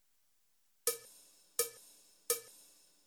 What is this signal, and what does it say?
There is no code.